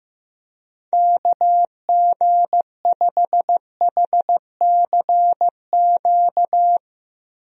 Morse code, heard as a keyed tone, about 15 wpm, 700 Hz. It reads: KG5HCQ
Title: USA Call Signs - 6 Characters - Repeated 2x - 15wpm